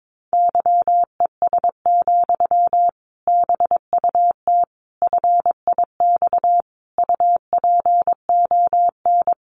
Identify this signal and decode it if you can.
Morse code, heard as a keyed tone, about 22 wpm, 700 Hz. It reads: YES, BUT FIX UPON
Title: Sentences from Top 1400 Words 22wpm